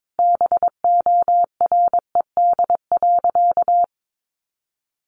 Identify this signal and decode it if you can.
Morse code, heard as a keyed tone, about 22 wpm, 700 Hz. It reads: BORED.
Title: Easy Phrases 22wpm